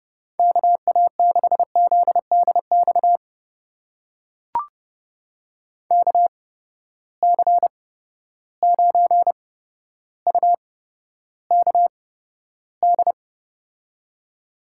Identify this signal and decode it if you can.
Morse code, heard as a keyed tone, about 30 wpm, spaced as for 13 wpm, 700 Hz. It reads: KA6ZDX KC9UKD
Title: USA Call Signs - 6 Characters - 8x Character Spacing - 30wpm